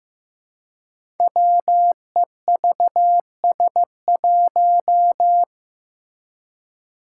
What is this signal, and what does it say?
Morse code, heard as a keyed tone, about 15 wpm, 700 Hz. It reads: WEVS1
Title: Five Challenging Letter-Number Combinations 15wpm